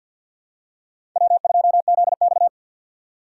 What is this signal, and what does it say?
Morse code, heard as a keyed tone, about 53 wpm, 700 Hz. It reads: W2ZX
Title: USA Call Signs - 4 Characters - 3x Character Spacing - 50wpm